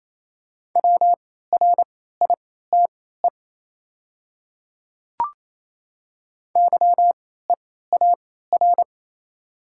Morse code, heard as a keyed tone, about 28 wpm, spaced as for 16 wpm, 700 Hz. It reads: WRITE YEAR
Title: 3x Character Spacing - Top 200 Words - 28wpm